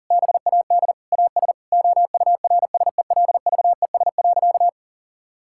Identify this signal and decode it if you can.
Morse code, heard as a keyed tone, about 40 wpm, 700 Hz. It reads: BAD AS OURSELVES.
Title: Sentences from Top 1600 Words 40wpm